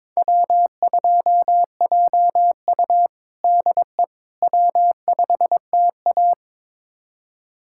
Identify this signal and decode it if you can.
Morse code, heard as a keyed tone, about 22 wpm, 700 Hz. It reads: W2JU DE W5TA
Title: USA Callsign DE Callsign - 4 Characters - 22wpm